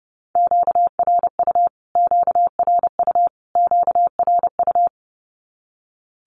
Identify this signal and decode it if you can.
Morse code, heard as a keyed tone, about 30 wpm, 700 Hz. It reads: QRU QRU QRU
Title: Q-Codes Repeated 3x 30wpm